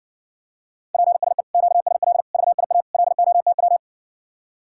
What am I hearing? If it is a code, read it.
Morse code, heard as a keyed tone, about 60 wpm, 700 Hz. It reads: PSE QSL VIA LOTW